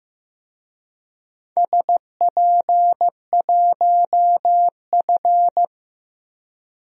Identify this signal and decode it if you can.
Morse code, heard as a keyed tone, about 15 wpm, 700 Hz. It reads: SP1F
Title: DXCC-100 Call Signs - 4 Characters - 15wpm